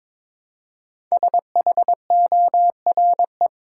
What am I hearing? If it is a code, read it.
Morse code, heard as a keyed tone, about 22 wpm, 700 Hz. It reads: SHORE
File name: 5 Letter Words - First 3 Letters Given - 22wpm